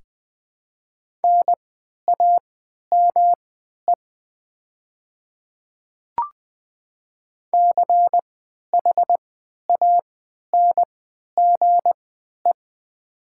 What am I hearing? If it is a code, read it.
Morse code, heard as a keyed tone, about 21 wpm, spaced as for 11 wpm, 700 Hz. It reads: NAME CHANGE
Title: 3x Character Spacing - Top 200 Words - 20wpm